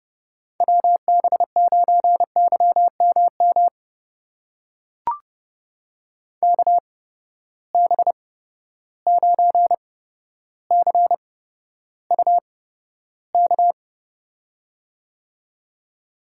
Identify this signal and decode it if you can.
Morse code, heard as a keyed tone, about 30 wpm, spaced as for 13 wpm, 700 Hz. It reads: WB9YMM KB9CUK
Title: USA Call Signs - 6 Characters - 8x Character Spacing - 30wpm